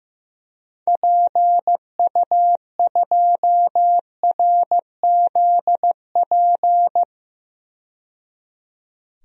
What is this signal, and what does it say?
Morse code, heard as a keyed tone, about 15 wpm, 700 Hz. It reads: PU2RZP
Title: DXCC-100 Call Signs - 6 Characters - 15wpm